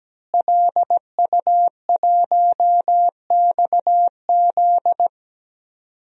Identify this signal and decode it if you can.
Morse code, heard as a keyed tone, about 17 wpm, 700 Hz. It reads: LU1XZ
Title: DXCC-100 Call Signs - 5 Characters - Repeated 2x - 17wpm